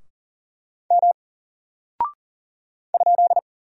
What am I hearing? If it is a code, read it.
Morse code, heard as a keyed tone, about 40 wpm, 700 Hz. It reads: M ?
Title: Single Character (Letter:Number:Punctuation:Prosign) 40wpm